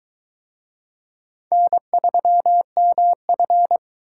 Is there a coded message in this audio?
Yes: Morse code from 1.5 to 3.8 s, heard as a keyed tone, about 22 wpm, 700 Hz. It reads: N3MF